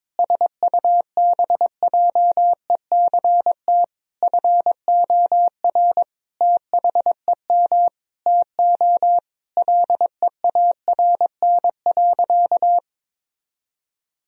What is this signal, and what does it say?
Morse code, heard as a keyed tone, about 21 wpm, 700 Hz. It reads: SUBJECT FOR THEM TO LEARN.
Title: Sentences from Top 1000 Words - Review Part 3 - 22wpm